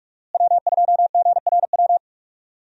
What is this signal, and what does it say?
Morse code, heard as a keyed tone, about 46 wpm, 700 Hz. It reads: W2GRW
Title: USA Call Signs - 5 Characters - Repeated 3x - 45wpm